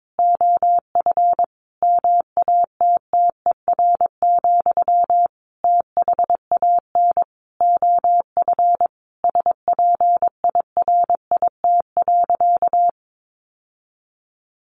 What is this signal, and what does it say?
Morse code, heard as a keyed tone, about 22 wpm, 700 Hz. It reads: OF MATTER, THAN OF SPIRIT.
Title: Sentences from Top 1400 Words 22wpm